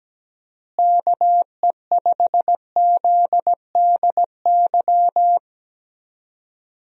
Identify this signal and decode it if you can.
Morse code, heard as a keyed tone, about 17 wpm, 700 Hz. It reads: KE5ZDY
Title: USA Call Signs - 6 Characters - Repeated 3x - 17wpm